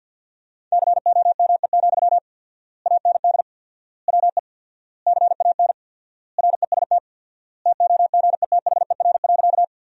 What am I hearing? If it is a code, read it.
Morse code, heard as a keyed tone, about 50 wpm, 700 Hz. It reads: COME, AND WE CAN REST TOGETHER.
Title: 4x Word-Spacing - Sentences from Top 500 Words - 50wpm